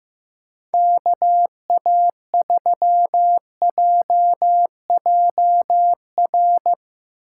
Morse code, heard as a keyed tone, about 15 wpm, 700 Hz. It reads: KA3JJR
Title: USA Call Signs - 6 Characters - 4x Character Spacing - 15wpm